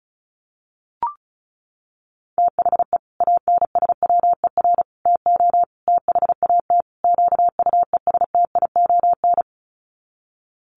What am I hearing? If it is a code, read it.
Morse code, heard as a keyed tone, about 34 wpm, 700 Hz. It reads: THE ANSWER TO THAT QUESTION
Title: Sets of 5 Words Encore2 35wpm